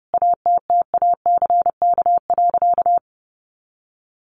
Morse code, heard as a keyed tone, about 30 wpm, 700 Hz. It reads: ATTACK.
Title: Sentences from Top 1500 Words 30wpm